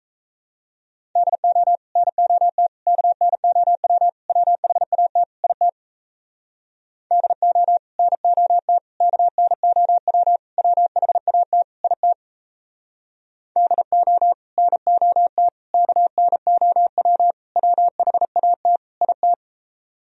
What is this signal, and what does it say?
Morse code, heard as a keyed tone, about 38 wpm, 700 Hz. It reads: DO NOT KNOW WHAT IT DO NOT KNOW WHAT IT DO NOT KNOW WHAT IT